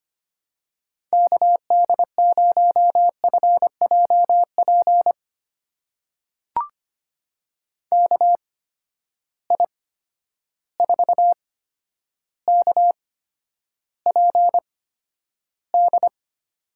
Morse code, heard as a keyed tone, about 25 wpm, spaced as for 11 wpm, 700 Hz. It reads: KD0FJP KI4KPD
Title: USA Call Signs - 6 Characters - 8x Character Spacing - 25wpm